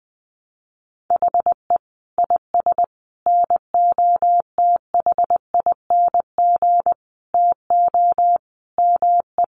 Morse code, heard as a keyed tone, about 20 wpm, 700 Hz. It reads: HE IS NOTHING TO ME
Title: Sentences from Top 1200 Words 20wpm